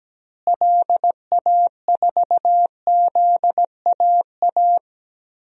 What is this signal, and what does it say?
Morse code, heard as a keyed tone, about 17 wpm, 700 Hz. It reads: LA4ZAA
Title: DXCC-100 Call Signs - 6 Characters - Repeated 2x - 17wpm